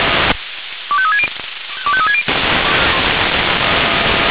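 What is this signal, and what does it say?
Five-tone selective call (ZVEI), heard as quick runs of five tones, tones of about 70 ms. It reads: 25360 25360
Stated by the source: ZVEI five-tone selective call received with an SDR.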